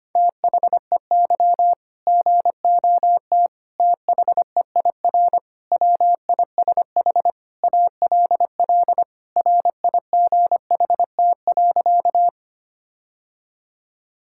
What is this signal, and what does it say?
Morse code, heard as a keyed tone, about 25 wpm, 700 Hz. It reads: THEY GOT THEIR WISH ALL RIGHT.
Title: Sentences from Top 1600 Words 25wpm